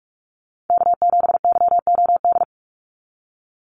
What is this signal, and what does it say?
Morse code, heard as a keyed tone, about 44 wpm, 700 Hz. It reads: K7YOD
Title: USA Call Signs 45wpm